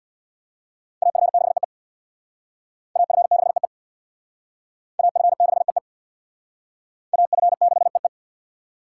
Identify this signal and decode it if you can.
Morse code, heard as a keyed tone, about 61 wpm, 700 Hz. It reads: AF6EE AF6EE AF6EE AF6EE
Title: Speed Racing - USA 2x2 Call Signs 50wpm